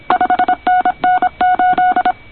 Morse code, heard as a keyed tone, about 26 wpm, 1470 Hz, with noise. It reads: 5NN 8
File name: co8tw / cqwwcw2010